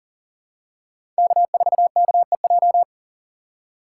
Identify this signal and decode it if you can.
Morse code, heard as a keyed tone, about 40 wpm, 700 Hz. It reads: K4KEJ
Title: USA Call Signs - 4x Character Spacing - 40wpm